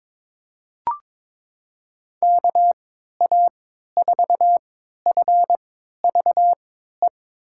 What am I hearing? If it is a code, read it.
Morse code, heard as a keyed tone, about 22 wpm, spaced as for 13 wpm, 700 Hz. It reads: KA4FVE
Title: USA Call Signs - 6 Characters - 3x Character Spacing - 22wpm